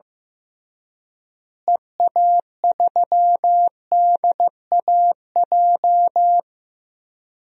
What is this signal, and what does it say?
Morse code, heard as a keyed tone, about 15 wpm, 700 Hz. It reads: EA3DAJ
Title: DXCC-100 Call Signs - 6 Characters - Repeated 3x - 15wpm